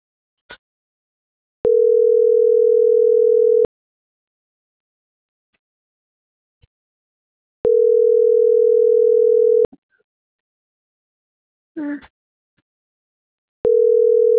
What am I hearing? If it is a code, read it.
Ringback tone (North America), 440 + 480 Hz.